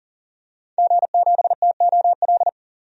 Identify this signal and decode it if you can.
Morse code, heard as a keyed tone, about 38 wpm, 700 Hz. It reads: G7TOL
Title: DXCC-100 Call Signs - 5 Characters - 40wpm